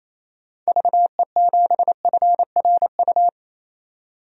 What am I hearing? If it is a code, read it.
Morse code, heard as a keyed tone, about 27 wpm, 700 Hz. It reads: VE7FRU
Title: DXCC-100 Call Signs - 6 Characters - 28wpm